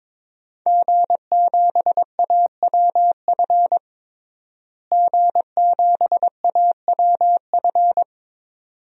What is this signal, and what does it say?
Morse code, heard as a keyed tone, about 22 wpm, 700 Hz. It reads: G7AWF G7AWF
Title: DXCC-100 Call Signs - 5 Characters - Repeated 2x - 22wpm